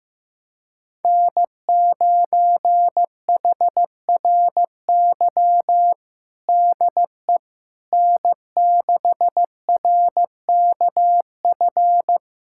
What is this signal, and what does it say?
Morse code, heard as a keyed tone, about 16 wpm, 700 Hz. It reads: N9HRY DE N6RKF
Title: USA Callsign DE Callsign - 5 Characters - 15wpm